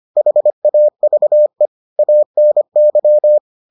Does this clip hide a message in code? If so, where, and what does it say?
Yes: Morse code from 0.2 to 3.4 s, heard as a keyed tone, about 25 wpm, 600 Hz. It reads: HAVE ANY